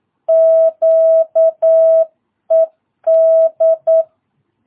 Morse code, heard as a keyed tone, about 9 wpm, 660 Hz. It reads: QED
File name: jh8cbh / 2022cw